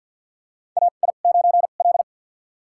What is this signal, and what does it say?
Morse code, heard as a keyed tone, about 51 wpm, spaced as for 36 wpm, 700 Hz. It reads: AI9L
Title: USA Call Signs - 4 Characters - 2x Character Spacing - 50wpm